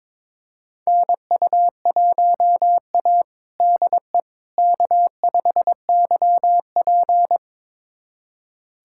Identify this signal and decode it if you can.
Morse code, heard as a keyed tone, about 22 wpm, 700 Hz. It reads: NU1A DE K5YP
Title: USA Callsign DE Callsign - 4 Characters - 22wpm